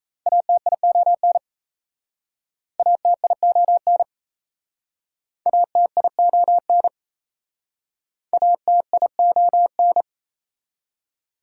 Morse code, heard as a keyed tone, about 37 wpm, 690 Hz. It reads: ATION ATION ATION ATION